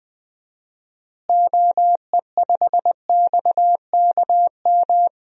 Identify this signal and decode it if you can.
Morse code, heard as a keyed tone, about 20 wpm, 700 Hz. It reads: OE5XKM
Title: DXCC-100 Call Signs - 6 Characters - Repeated 3x - 20wpm